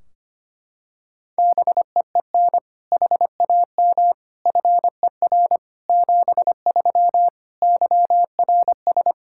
Morse code, heard as a keyed tone, about 25 wpm, 700 Hz. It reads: BEEN HAM FER 73 YRS
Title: Practice QSO - Exchange 3 - More Info - 25wpm